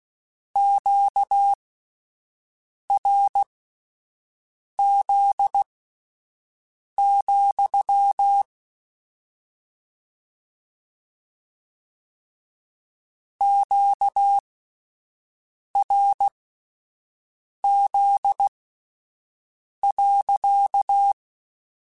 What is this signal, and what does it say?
Morse code, heard as a keyed tone, about 16 wpm, spaced as for 5 wpm, 800 Hz. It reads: QRZ, QRZ.
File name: Q Calls / Q Calls-mp3s / Q Calls sppc @10